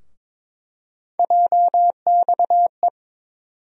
Morse code, heard as a keyed tone, about 21 wpm, 700 Hz. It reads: JXE
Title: USA Call Sign Postfixes - 3 Letters 22wpm